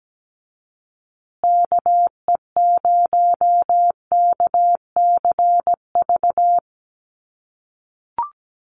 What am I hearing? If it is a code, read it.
Morse code, heard as a keyed tone, about 17 wpm, 700 Hz. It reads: KE0KCV
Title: USA Call Signs Repeated 3x 17wpm